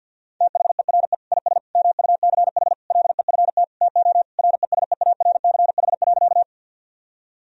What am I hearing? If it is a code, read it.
Morse code, heard as a keyed tone, about 50 wpm, 700 Hz. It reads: THERE IS MUCH LEFT TO RESEARCH.
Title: Sentences from Top 1000 Words - Review Part 4 - 50wpm